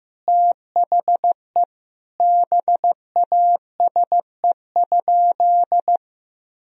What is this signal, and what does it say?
Morse code, heard as a keyed tone, about 15 wpm, 700 Hz. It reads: THE BASE?
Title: Sentences from Top 1000 Words - Review Part 1 - 15wpm